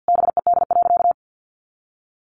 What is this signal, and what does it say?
Morse code, heard as a keyed tone, about 50 wpm, 700 Hz. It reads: BED.
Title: Sentences from Top 500 Words 50wpm